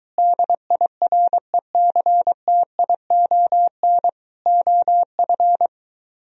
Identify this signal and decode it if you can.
Morse code, heard as a keyed tone, about 23 wpm, 700 Hz. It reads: DIRECTION OF